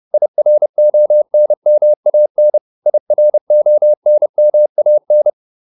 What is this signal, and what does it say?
Morse code, heard as a keyed tone, about 30 wpm, 600 Hz. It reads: IRONMAN IRONMAN